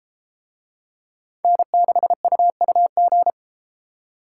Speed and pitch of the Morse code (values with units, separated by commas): 32 wpm, 700 Hz